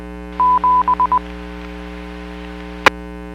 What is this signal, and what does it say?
Morse code, heard as a keyed tone, about 20 wpm, 1000 Hz. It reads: 7 E
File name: WPS / 01497 12-50 WPS 451.25 WPS